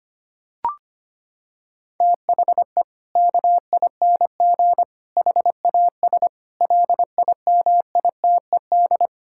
Morse code, heard as a keyed tone, about 25 wpm, 700 Hz. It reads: THE KING HAS LIMITED
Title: Sentences from Top 1000 Words - Review Part 6 - 25wpm